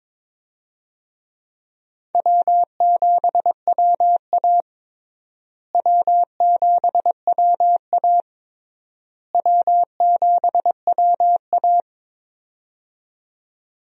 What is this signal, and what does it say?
Morse code, heard as a keyed tone, about 22 wpm, 700 Hz. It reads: W7WA W7WA W7WA
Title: USA Call Signs - 4 Characters - Repeated 3x - 22wpm